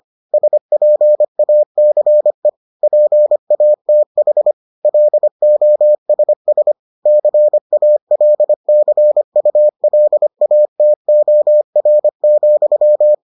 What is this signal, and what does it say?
Morse code, heard as a keyed tone, about 25 wpm, 600 Hz. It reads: SPACE PATH LOSS CALCULATOR,